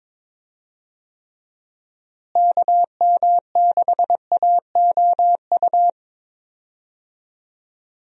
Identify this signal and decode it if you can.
Morse code, heard as a keyed tone, about 22 wpm, 700 Hz. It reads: KM6AOU
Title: USA Call Signs - 6 Characters - Repeated 3x - 22wpm